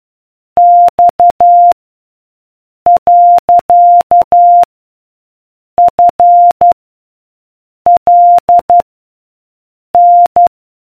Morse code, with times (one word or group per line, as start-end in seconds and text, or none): 0.6-1.7 X
2.9-4.6 .
5.8-6.7 F
7.9-8.8 L
9.9-10.5 N